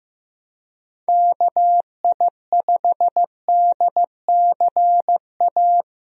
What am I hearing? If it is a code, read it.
Morse code, heard as a keyed tone, about 15 wpm, 700 Hz. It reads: KI5DCA
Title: USA Call Signs - 6 Characters - 15wpm